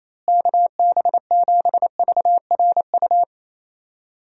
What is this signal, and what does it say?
Morse code, heard as a keyed tone, about 27 wpm, 700 Hz. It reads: KB7VRU
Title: USA Call Signs - 6 Characters - Repeated 2x - 28wpm